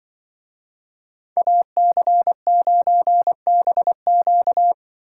Morse code, heard as a keyed tone, about 24 wpm, 700 Hz. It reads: AC9BQ
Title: Speed Racing - USA Call Signs 20wpm